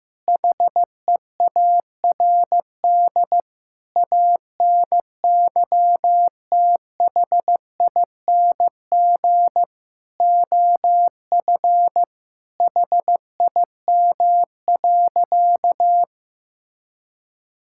Morse code, heard as a keyed tone, about 16 wpm, 700 Hz. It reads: HEARD ANYTHING OF HIM.
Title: Sentences from Top 1600 Words 15wpm